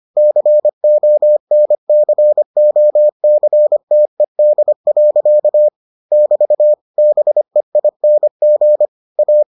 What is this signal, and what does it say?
Morse code, heard as a keyed tone, about 25 wpm, 600 Hz. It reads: CONCOCTED. <BT> BEING A